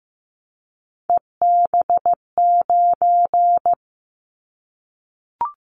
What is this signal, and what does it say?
Morse code, heard as a keyed tone, about 15 wpm, 700 Hz. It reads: EB9
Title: Callsign Prefixes for Top 100 DXCC - 15wpm